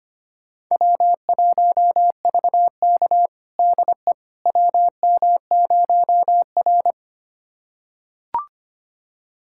Morse code, heard as a keyed tone, about 25 wpm, 700 Hz. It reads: W1VK DE WM0R
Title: USA Callsign DE Callsign - 4 Characters - 25wpm